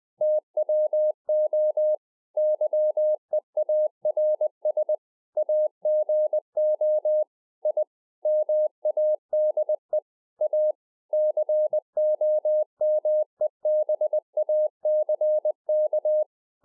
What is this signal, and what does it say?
Morse code, heard as a keyed tone, about 20 wpm, 610 Hz. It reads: TWO YEARS AGO I MADE A COMEBACK